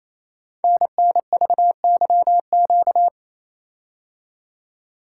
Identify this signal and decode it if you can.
Morse code, heard as a keyed tone, about 29 wpm, 700 Hz. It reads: NNVYQ
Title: Code Groups - 5 Letters - 28wpm